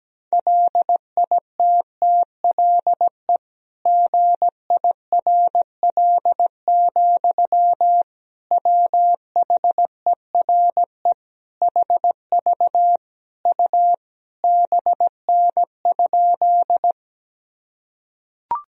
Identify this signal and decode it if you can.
Morse code, heard as a keyed tone, about 17 wpm, 700 Hz. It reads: LITTLE GIRL, WHERE HV U BN?
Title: Sentences from Top 400 Words and Abbreviations 17wpm